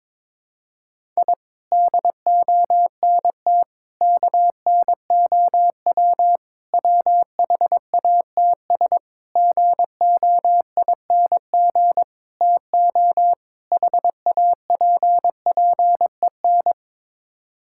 Morse code, heard as a keyed tone, about 22 wpm, 700 Hz. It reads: I DONT KNOW WHATS GOING TO HAPPEN
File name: practice / 22wpm / Sets of 7 Words - 22wpm